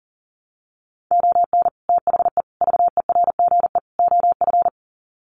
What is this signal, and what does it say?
Morse code, heard as a keyed tone, about 37 wpm, 700 Hz. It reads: ON THE VERGE OF
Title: Sets of 4 Words 40wpm